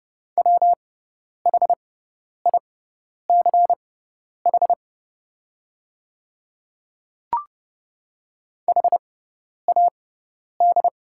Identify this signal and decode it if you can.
Morse code, heard as a keyed tone, about 30 wpm, spaced as for 10 wpm, 700 Hz. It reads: WHICH HAD
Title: 6x Character Spacing - Top 100 Words - 30wpm